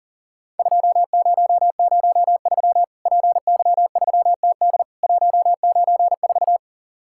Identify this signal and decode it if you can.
Morse code, heard as a keyed tone, about 40 wpm, 700 Hz. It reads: 2003 PY3TD 194
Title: Worked All Europe DX Contest - QTC Exchange - 40wpm